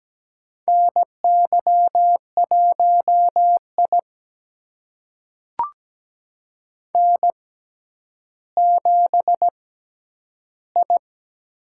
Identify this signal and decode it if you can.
Morse code, heard as a keyed tone, about 17 wpm, spaced as for 10 wpm, 700 Hz. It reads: NY1I N7I
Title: USA Call Signs - 4 Characters - 6x Character Spacing - 17wpm